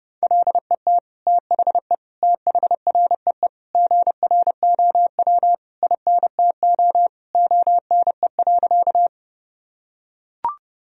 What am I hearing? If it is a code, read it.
Morse code, heard as a keyed tone, about 30 wpm, 700 Hz. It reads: LET THE THREE GROW INTO ONE.
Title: Sentences from Top 1000 Words - Review Part 1 - 30wpm